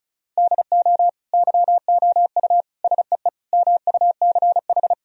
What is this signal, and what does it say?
Morse code, heard as a keyed tone, about 36 wpm, 700 Hz. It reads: DO YOU SEE MUCH
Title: Sentences from Top 1000 Words - Review Part 4 - 35wpm